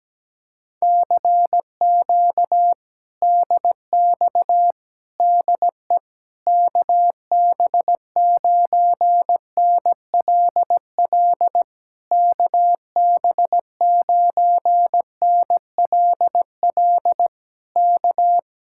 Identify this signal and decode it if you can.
Morse code, heard as a keyed tone, about 17 wpm, 700 Hz. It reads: CQ DX DE KB9NLL KB9NLL K
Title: Speed-Racing Common QSO Phrases Encore - 17wpm